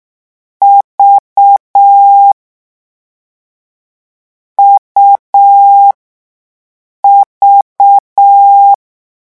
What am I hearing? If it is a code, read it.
Morse code, heard as a keyed tone, about 6 wpm, 800 Hz. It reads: V U V
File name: morse mp3 / lecon3-vitesse 6